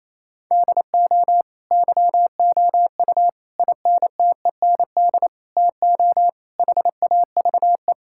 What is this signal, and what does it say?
Morse code, heard as a keyed tone, about 27 wpm, 700 Hz. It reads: DO YOU INTEND TO HAVE
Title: Sentences from Top 1600 Words 28wpm